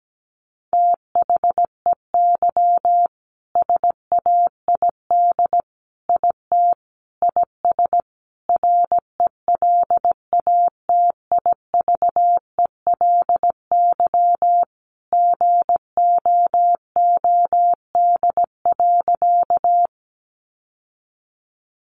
Morse code, heard as a keyed tone, about 17 wpm, 700 Hz. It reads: THEY SAID IT IS RELATIVELY GOOD.
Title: Sentences from Top 1100 Words 17wpm